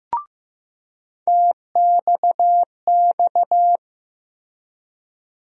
Morse code, heard as a keyed tone, about 15 wpm, 700 Hz. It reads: TXX
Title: USA Call Sign Postfixes - 3 Letters 15wpm